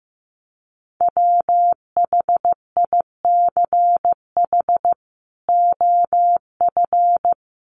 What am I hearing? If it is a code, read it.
Morse code, heard as a keyed tone, about 15 wpm, 700 Hz. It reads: WHICH OF
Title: Sentences from Top 1100 Words 15wpm